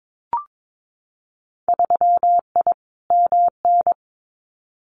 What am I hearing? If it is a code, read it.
Morse code, heard as a keyed tone, about 22 wpm, 700 Hz. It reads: 3I MN
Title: Winter Field Day Exchanges 22wpm